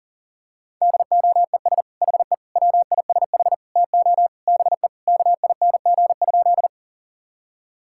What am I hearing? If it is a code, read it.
Morse code, heard as a keyed tone, about 40 wpm, 700 Hz. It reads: DOES HE WISH TO BE KING?
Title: Sentences from Top 1600 Words 40wpm